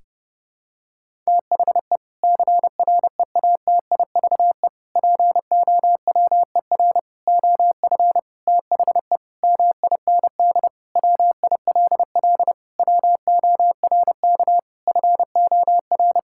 Morse code, heard as a keyed tone, about 32 wpm, 700 Hz. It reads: THE CREATIVE POWER OF THE MIND WILL WORK FOR
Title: Sentences from Top 1000 Words - Review Part 5 - 30wpm